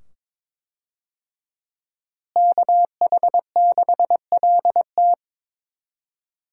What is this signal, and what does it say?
Morse code, heard as a keyed tone, about 22 wpm, 700 Hz. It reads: KH6LT
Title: USA Call Signs - 5 Characters - Repeated 2x - 22wpm